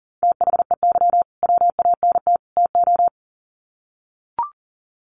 Morse code, heard as a keyed tone, about 40 wpm, 700 Hz. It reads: THEY WANT TO
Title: Sets of 3 Words 40wpm